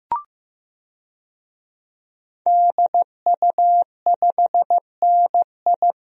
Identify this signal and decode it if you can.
Morse code, heard as a keyed tone, about 15 wpm, 700 Hz. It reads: DU5NI
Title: DXCC-100 Call Signs - 5 Characters - 15wpm